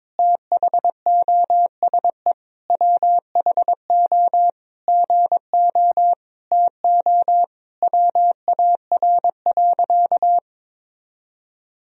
Morse code, heard as a keyed tone, about 22 wpm, 700 Hz. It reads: THOSE WHO GO TO WAR.
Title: Sentences from Top 1000 Words - Review Part 2 - 22wpm